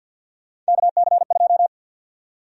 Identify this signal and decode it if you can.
Morse code, heard as a keyed tone, about 50 wpm, 700 Hz. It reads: KC2
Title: USA Call Sign Prefixes - 50wpm